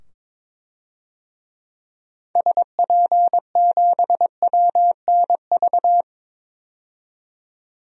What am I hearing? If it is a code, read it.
Morse code, heard as a keyed tone, about 22 wpm, 700 Hz. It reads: SP7WNV
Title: DXCC-100 Call Signs - 6 Characters - 22wpm